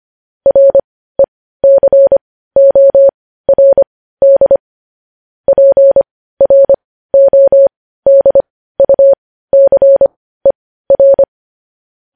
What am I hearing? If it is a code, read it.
Morse code, heard as a keyed tone, about 25 wpm, spaced as for 15 wpm, 550 Hz. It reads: RECORD PRODUCER
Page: news headlines in morse code at 15 wpm